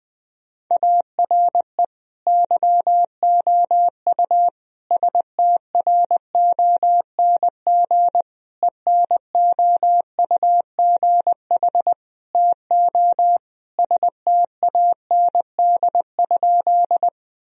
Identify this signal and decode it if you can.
Morse code, heard as a keyed tone, about 20 wpm, 700 Hz. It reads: ARE YOU STRONG ENOUGH TO STAND?
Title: Sentences from Top 1000 Words - Review Part 3 - 20wpm